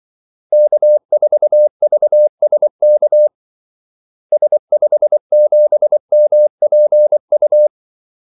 Morse code, heard as a keyed tone, about 24 wpm, 600 Hz. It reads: K4VSK S57MPU
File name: Callsigns-24wpm